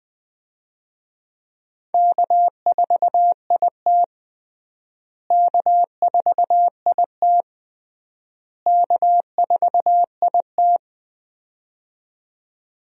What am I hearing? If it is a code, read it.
Morse code, heard as a keyed tone, about 20 wpm, 700 Hz. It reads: K4IT K4IT K4IT
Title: USA Call Signs - 4 Characters - Repeated 3x - 20wpm